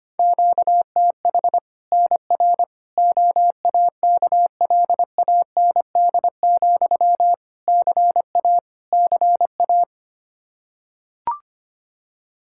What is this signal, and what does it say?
Morse code, heard as a keyed tone, about 25 wpm, 700 Hz. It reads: QTH NR OAKLAND, CA CA
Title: K1USN SST QSO Practice 25wpm